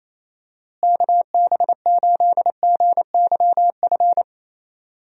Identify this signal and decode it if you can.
Morse code, heard as a keyed tone, about 28 wpm, 700 Hz. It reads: KB8GYF
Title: USA Call Signs - 6 Characters - Repeated 3x - 28wpm